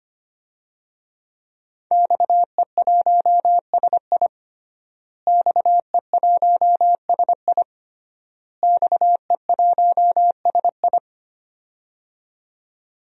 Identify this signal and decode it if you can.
Morse code, heard as a keyed tone, about 25 wpm, 700 Hz. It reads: XE1SI XE1SI XE1SI
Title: DXCC-100 Call Signs - 5 Characters - Repeated 3x - 25wpm